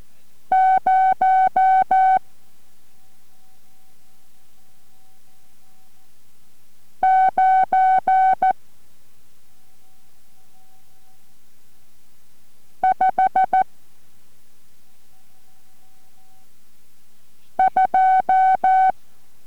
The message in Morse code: 0 9 5 2